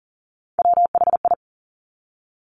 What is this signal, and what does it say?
Morse code, heard as a keyed tone, about 38 wpm, 690 Hz. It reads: WHI
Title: Top Three Letter Combinations in Words 40wpm